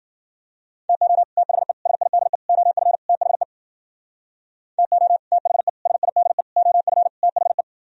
Morse code, heard as a keyed tone, about 50 wpm, 700 Hz. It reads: TO THE SIDE OF THE TO THE SIDE OF THE
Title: Speed-Racing - Sets of 5 Words 40wpm